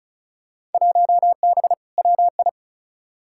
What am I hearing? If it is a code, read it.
Morse code, heard as a keyed tone, about 34 wpm, 700 Hz. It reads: 1B WI
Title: Field Day Exchanges - 35wpm